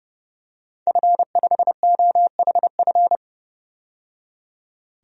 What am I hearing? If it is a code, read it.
Morse code, heard as a keyed tone, about 30 wpm, 700 Hz. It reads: F5OHF
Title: DXCC-100 Call Signs - 5 Characters - 30wpm